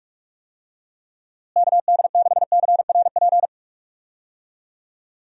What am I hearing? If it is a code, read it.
Morse code, heard as a keyed tone, about 46 wpm, 700 Hz. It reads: KD6CRP
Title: USA Call Signs - 6 Characters - Repeated 3x - 45wpm